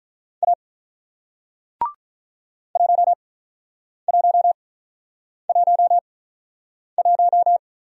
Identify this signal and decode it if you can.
Morse code, heard as a keyed tone, about 48 wpm, spaced as for 9 wpm, 690 Hz. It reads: A 1111